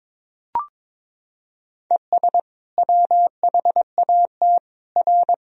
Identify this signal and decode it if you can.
Morse code, heard as a keyed tone, about 22 wpm, 700 Hz. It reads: ES WHAT R
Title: Sentences from Top 200 Words and Abbreviations 22wpm